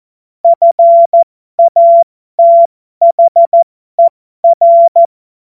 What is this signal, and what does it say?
Morse code, heard as a keyed tone, about 14 wpm, spaced as for 12 wpm, 680 Hz. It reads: FATHER
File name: Morsekurs / Witze